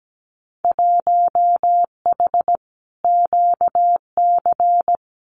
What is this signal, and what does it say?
Morse code, heard as a keyed tone, about 17 wpm, 700 Hz. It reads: 1H QC
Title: Winter Field Day Exchanges 17wpm